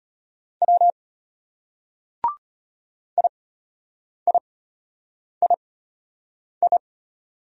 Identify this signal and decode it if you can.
Morse code, heard as a keyed tone, about 31 wpm, spaced as for 8 wpm, 700 Hz. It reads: W IIII
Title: Speed-Racing Single Letters 25wpm